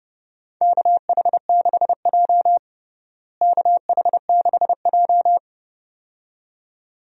Morse code, heard as a keyed tone, about 32 wpm, 700 Hz. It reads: KH6J KH6J
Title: DXCC-100 Call Signs - 4 Characters - Repeated 2x - 30wpm